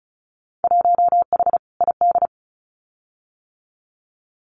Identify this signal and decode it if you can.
Morse code, heard as a keyed tone, about 36 wpm, 690 Hz. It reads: 1H ID
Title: Winter Field Day Exchanges 35wpm